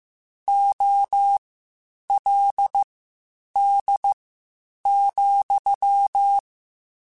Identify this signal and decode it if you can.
Morse code, heard as a keyed tone, about 15 wpm, spaced as for 8 wpm, 800 Hz. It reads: OLD,